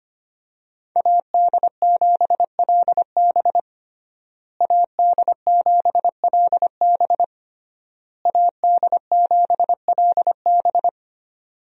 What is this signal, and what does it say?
Morse code, heard as a keyed tone, about 25 wpm, 700 Hz. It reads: AD7LB AD7LB AD7LB
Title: USA Call Signs - 5 Characters - Repeated 3x - 25wpm